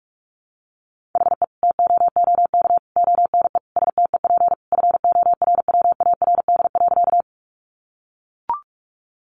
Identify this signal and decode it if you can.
Morse code, heard as a keyed tone, about 44 wpm, 700 Hz. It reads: HE TOOK ONE STEP FORWARD.
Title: Sentences from Top 1300 Words 45wpm